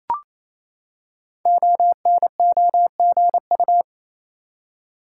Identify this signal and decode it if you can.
Morse code, heard as a keyed tone, about 29 wpm, 700 Hz. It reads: ONOGU
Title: Code Groups - 5 Letters - 28wpm